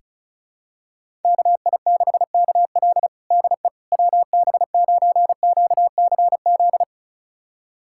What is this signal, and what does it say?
Morse code, heard as a keyed tone, about 34 wpm, 700 Hz. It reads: KI6KL DE WB9QCZ
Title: USA Callsign DE Callsign 35wpm